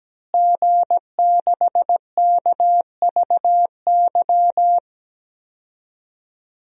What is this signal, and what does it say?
Morse code, heard as a keyed tone, about 17 wpm, 700 Hz. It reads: G6KVY
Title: DXCC-100 Call Signs - 5 Characters - Repeated 2x - 17wpm